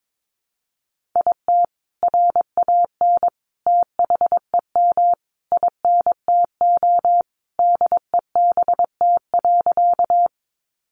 Morse code, heard as a keyed tone, about 22 wpm, 700 Hz. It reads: IT RAN THEM INTO DEBT.
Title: Sentences from Top 1300 Words 22wpm